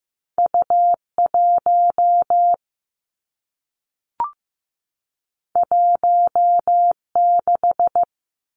Morse code, heard as a keyed tone, about 15 wpm, 700 Hz. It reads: U1 16
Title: Two Challenging Letter-Number Combinations 15wpm